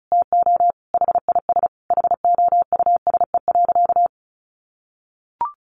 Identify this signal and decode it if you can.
Morse code, heard as a keyed tone, about 34 wpm, 700 Hz. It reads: TO HIS HOUSE.
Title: Additional Easy Phrases 35wpm